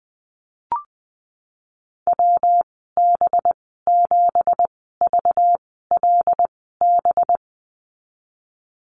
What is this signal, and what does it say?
Morse code, heard as a keyed tone, about 20 wpm, spaced as for 15 wpm, 700 Hz. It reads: WB7VLB
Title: 193 Lesson 31g - Letter v - Call Signs 20wpm